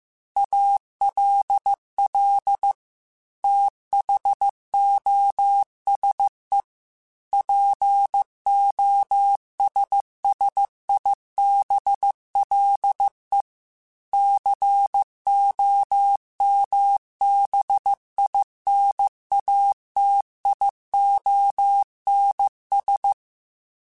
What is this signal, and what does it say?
Morse code, heard as a keyed tone, about 15 wpm, 800 Hz. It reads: ALL THOSE POSSIBLE COMBINATIONS
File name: Technical Subjects / Technical Subjects-mp3s / AND Gate @13